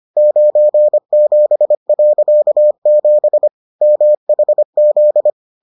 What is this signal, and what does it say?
Morse code, heard as a keyed tone, about 25 wpm, 600 Hz. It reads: 97.7 MHZ